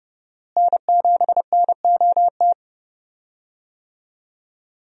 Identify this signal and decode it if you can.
Morse code, heard as a keyed tone, about 30 wpm, 700 Hz. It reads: N7NOT
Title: USA Call Signs - 5 Characters - Repeated 3x - 30wpm